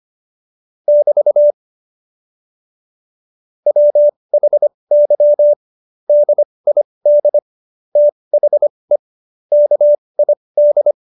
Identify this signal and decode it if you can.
Morse code, heard as a keyed tone, about 25 wpm, spaced as for 20 wpm, 590 Hz. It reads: <BT> WHY DID THE KID
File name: Jokes-25-char-20-eff